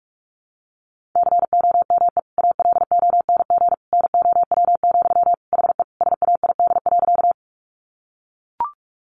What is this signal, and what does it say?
Morse code, heard as a keyed tone, about 44 wpm, 700 Hz. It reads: COME ALONG NOW, HE SAID.
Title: Sentences from Top 700 Words 45wpm